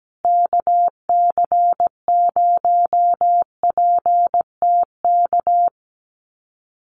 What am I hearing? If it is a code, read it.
Morse code, heard as a keyed tone, about 17 wpm, 700 Hz. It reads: KC0PTK
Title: USA Call Signs Repeated 2x 17wpm